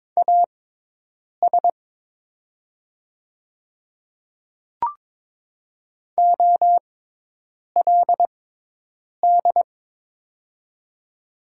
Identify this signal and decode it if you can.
Morse code, heard as a keyed tone, about 22 wpm, spaced as for 8 wpm, 700 Hz. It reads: AS OLD
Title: 6x Character Spacing - Top 300 Words - 22wpm